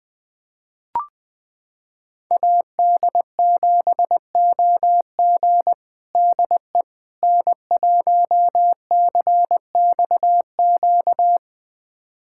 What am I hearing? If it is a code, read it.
Morse code, heard as a keyed tone, about 20 wpm, 700 Hz. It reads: AD7OG DE N1CXQ
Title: USA Callsign DE Callsign - 5 Characters - 20wpm